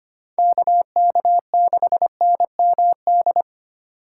Morse code, heard as a keyed tone, about 29 wpm, 700 Hz. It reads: KK6NMD
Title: Speed Racing - USA Call Signs 25wpm